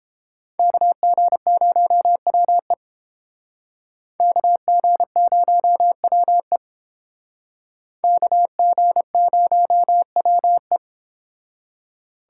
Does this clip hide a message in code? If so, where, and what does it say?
Yes: Morse code from 0.6 to 10.8 s, heard as a keyed tone, about 29 wpm, 700 Hz. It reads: KG0WE KG0WE KG0WE